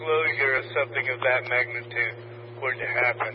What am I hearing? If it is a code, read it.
Emergency Alert System (SAME) tones, heard as harsh buzzing bursts, with speech. It is heard from 0.2 to 3.0 s.